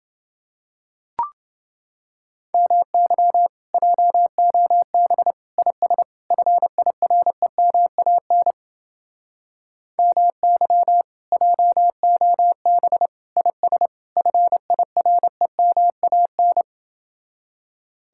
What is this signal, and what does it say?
Morse code, heard as a keyed tone, about 30 wpm, 700 Hz. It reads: MY JOB IS FIREMAN MY JOB IS FIREMAN